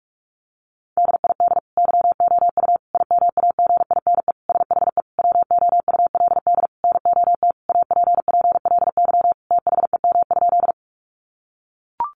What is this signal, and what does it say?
Morse code, heard as a keyed tone, about 44 wpm, 700 Hz. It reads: DID YOU IMAGINE SHE WOULD NOT APPLY THEM?
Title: Sentences from Top 1400 Words 45wpm